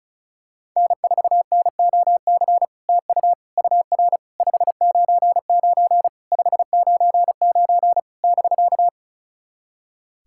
Morse code, heard as a keyed tone, about 36 wpm, 660 Hz. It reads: N4NOC TU UR 599 599 <BK>
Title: SOTA QSO Practice 35wpm